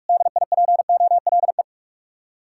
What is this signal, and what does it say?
Morse code, heard as a keyed tone, about 59 wpm, 700 Hz. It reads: DIPOLE